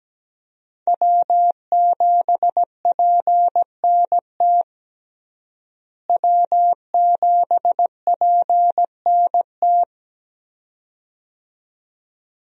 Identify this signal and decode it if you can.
Morse code, heard as a keyed tone, about 17 wpm, 700 Hz. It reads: W7PNT W7PNT
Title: USA Call Signs - 5 Characters - Repeated 2x - 17wpm